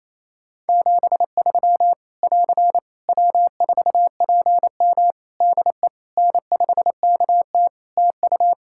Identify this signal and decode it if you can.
Morse code, heard as a keyed tone, about 28 wpm, 700 Hz. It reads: 73 <AR> W4PM DE N5KT TU
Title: K1USN SST QSO Practice 28wpm